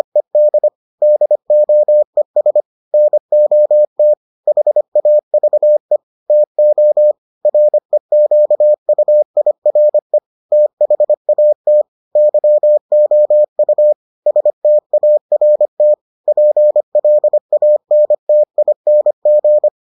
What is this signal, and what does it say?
Morse code, heard as a keyed tone, about 25 wpm, 600 Hz. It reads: ED DOES NOT HAVE TO REQUIRE THAT YOU START PLANTING